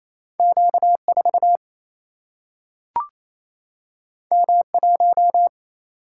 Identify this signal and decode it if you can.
Morse code, heard as a keyed tone, about 28 wpm, 700 Hz. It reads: Q 4 M 1
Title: Single-Letter Single-Number 28wpm